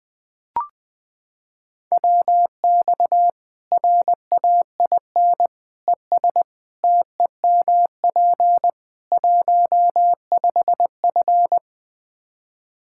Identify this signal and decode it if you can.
Morse code, heard as a keyed tone, about 21 wpm, 700 Hz. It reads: WX RAIN ES TEMP 15F
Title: Practice QSO - Exchange 2 - Info - 20wpm